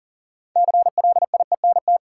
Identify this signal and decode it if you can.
Morse code, heard as a keyed tone, about 40 wpm, 700 Hz. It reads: CLIENT
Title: Six Letter Words - 40wpm